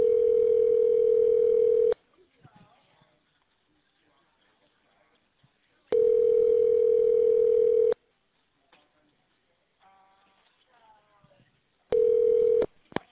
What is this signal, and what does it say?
Ringback tone (North America), 440 + 480 Hz.